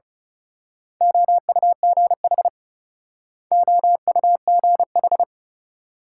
Morse code, heard as a keyed tone, about 33 wpm, 700 Hz. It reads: OUGH OUGH
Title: Speed-Racing Multi-letter Phonemes - 30wpm